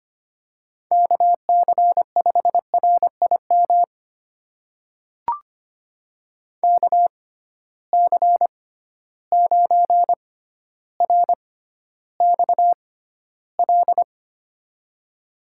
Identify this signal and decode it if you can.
Morse code, heard as a keyed tone, about 25 wpm, spaced as for 13 wpm, 700 Hz. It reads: KC5RIM KC9RXL
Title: USA Call Signs - 6 Characters - 6x Character Spacing - 25wpm